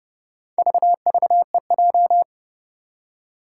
Morse code, heard as a keyed tone, about 29 wpm, 690 Hz. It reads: VVEJ
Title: Four Challenging Letter-Number Combinations 30wpm